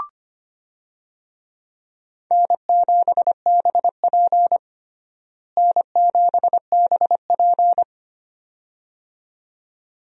Morse code, heard as a keyed tone, about 25 wpm, 700 Hz. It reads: N7BP N7BP
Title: USA Call Signs - 4 Characters - Repeated 2x - 25wpm